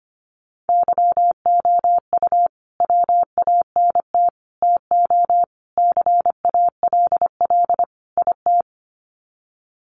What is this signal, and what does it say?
Morse code, heard as a keyed tone, about 25 wpm, 700 Hz. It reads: YOU WANT TO CALL IT
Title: Sets of 5 Words Encore2 25wpm